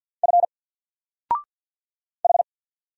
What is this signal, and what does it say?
Morse code, heard as a keyed tone, about 48 wpm, 700 Hz. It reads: F H
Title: Single Character (Letter:Number:Punctuation:Prosign) Alt 50wpm